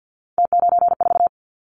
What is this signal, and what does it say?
Morse code, heard as a keyed tone, about 50 wpm, 700 Hz. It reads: T94
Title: Callsign Prefixes for Top 50 DXCC - 50wpm